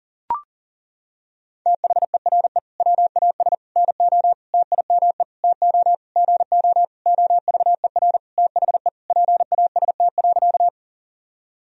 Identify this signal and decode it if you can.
Morse code, heard as a keyed tone, about 40 wpm, 700 Hz. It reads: THERE WAS NO TIME TO GO OVER THE PAST.
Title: Sentences from Top 1000 Words - Review Part 3 - 40wpm